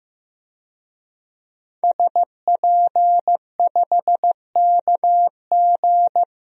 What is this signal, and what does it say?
Morse code, heard as a keyed tone, about 15 wpm, 700 Hz. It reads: SP5KG
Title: DXCC-100 Call Signs - 5 Characters - Repeated 2x - 15wpm